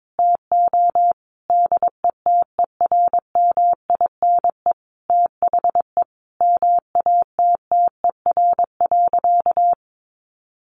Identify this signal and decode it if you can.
Morse code, heard as a keyed tone, about 22 wpm, 700 Hz. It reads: TO DETERMINE THE MATTER.
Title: Sentences from Top 1300 Words 22wpm